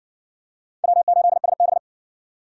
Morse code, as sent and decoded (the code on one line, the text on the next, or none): .-- ---.. ... -...
W8SB